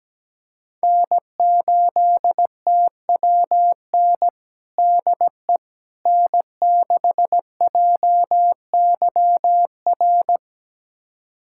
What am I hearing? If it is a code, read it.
Morse code, heard as a keyed tone, about 17 wpm, 700 Hz. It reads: N8TWN DE N6JYR
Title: USA Callsign DE Callsign 17wpm